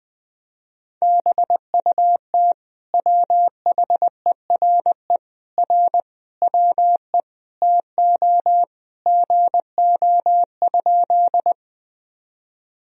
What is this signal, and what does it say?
Morse code, heard as a keyed tone, about 21 wpm, 700 Hz. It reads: BUT WHERE R WE TO GO?
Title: Sentences from Top 200 Words and Abbreviations 20wpm